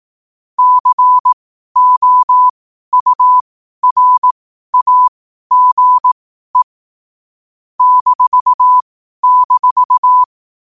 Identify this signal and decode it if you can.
Morse code, heard as a keyed tone, about 18 wpm, spaced as for 12 wpm, 1000 Hz. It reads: COURAGE --
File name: qotdfiles / 12wpm